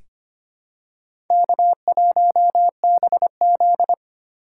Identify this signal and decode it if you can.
Morse code, heard as a keyed tone, about 25 wpm, 700 Hz. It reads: K1BZ
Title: K1USN SST QSO Practice - 25wpm